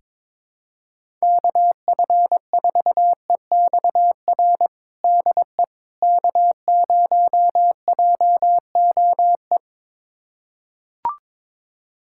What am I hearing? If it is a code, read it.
Morse code, heard as a keyed tone, about 22 wpm, 700 Hz. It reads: KF4EXR DE K0JOE
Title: USA Callsign DE Callsign 22wpm